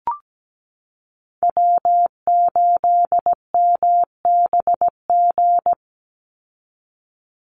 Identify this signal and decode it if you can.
Morse code, heard as a keyed tone, about 17 wpm, 700 Hz. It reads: W8MBG
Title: USA Call Signs 17wpm